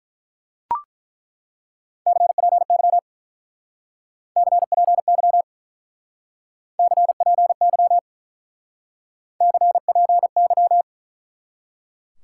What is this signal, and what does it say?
Morse code, heard as a keyed tone, about 42 wpm, 700 Hz. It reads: CPY CPY CPY CPY
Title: Speed-Racing Top 90 QSO Elements 35wpm